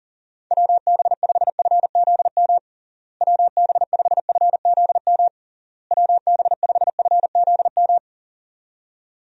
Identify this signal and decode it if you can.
Morse code, heard as a keyed tone, about 40 wpm, 700 Hz. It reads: WB5FZM WB5FZM WB5FZM
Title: USA Call Signs - 6 Characters - Repeated 3x - 40wpm